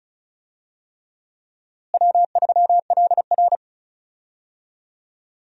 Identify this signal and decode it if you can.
Morse code, heard as a keyed tone, about 36 wpm, 700 Hz. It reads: W3LR
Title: USA Call Signs - 4 Characters - Repeated 3x - 35wpm